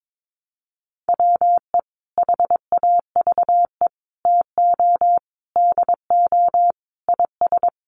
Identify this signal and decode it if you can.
Morse code, heard as a keyed tone, about 22 wpm, 700 Hz. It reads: WE HAVE TO DO IS
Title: Sets of 5 Words Encore 22wpm